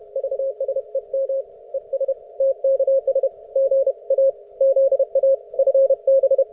Morse code, heard as a keyed tone, about 31 wpm, 540 Hz. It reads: VSEM ES TKS GA ZA FB